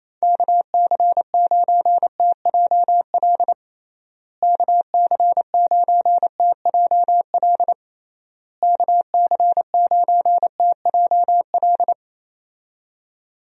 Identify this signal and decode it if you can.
Morse code, heard as a keyed tone, about 27 wpm, 700 Hz. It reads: KC9TJL KC9TJL KC9TJL
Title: USA Call Signs - 6 Characters - Repeated 3x - 28wpm